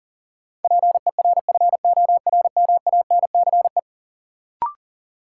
Morse code, heard as a keyed tone, about 40 wpm, 700 Hz. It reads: PERFORMANCE
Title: Sentences from Top 1000 Words - New Word Drill - 40wpm